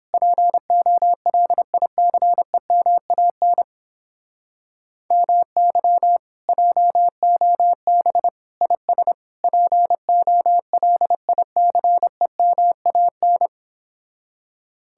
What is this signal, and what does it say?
Morse code, heard as a keyed tone, about 31 wpm, 700 Hz. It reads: POLICEMAN MY JOB IS POLICEMAN